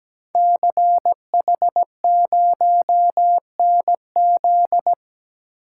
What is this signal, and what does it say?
Morse code, heard as a keyed tone, about 17 wpm, 700 Hz. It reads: CH0NZ
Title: Code Groups - 5 Letter:Numbers - 17wpm